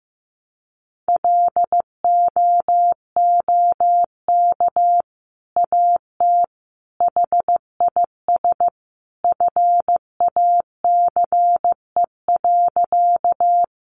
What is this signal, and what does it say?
Morse code, heard as a keyed tone, about 15 wpm, 700 Hz. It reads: LOOK AT HIS FACE.
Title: Sentences from Top 300 Words 15wpm